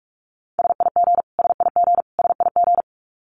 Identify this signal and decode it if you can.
Morse code, heard as a keyed tone, about 44 wpm, 700 Hz. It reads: SIG SIG SIG
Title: Abbreviations Repeated 3x - 45wpm